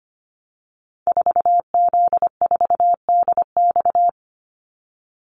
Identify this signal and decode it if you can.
Morse code, heard as a keyed tone, about 25 wpm, 700 Hz. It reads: 4Z4DX
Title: Call Signs Repeated 3x - 25wpm